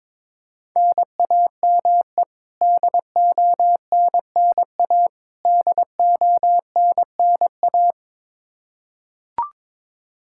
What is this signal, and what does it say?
Morse code, heard as a keyed tone, about 22 wpm, 700 Hz. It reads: NAME DONNA DONNA
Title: Practice QSO - Exchange 1 - Beginning - 22wpm